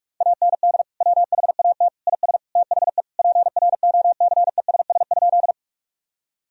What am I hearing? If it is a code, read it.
Morse code, heard as a keyed tone, about 41 wpm, 700 Hz. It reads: AND WHAT IS THE PROCESS?
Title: Sentences from Top 1000 Words - Review Part 3 - 45wpm